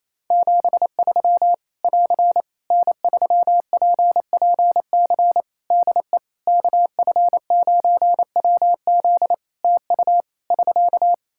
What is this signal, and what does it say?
Morse code, heard as a keyed tone, about 28 wpm, 700 Hz. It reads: 73 <AR> N3PPC DE KF9WZ TU <SK>
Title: K1USN SST QSO Practice 28wpm